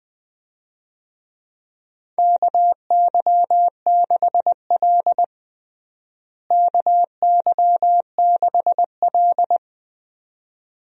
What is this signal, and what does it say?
Morse code, heard as a keyed tone, about 20 wpm, 700 Hz. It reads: KY6L KY6L
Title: USA Call Signs - 4 Characters - Repeated 2x - 20wpm